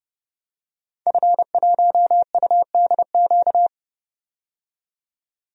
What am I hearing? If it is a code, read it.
Morse code, heard as a keyed tone, about 32 wpm, 700 Hz. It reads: F1UDQ
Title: DXCC-100 Call Signs - 5 Characters - 30wpm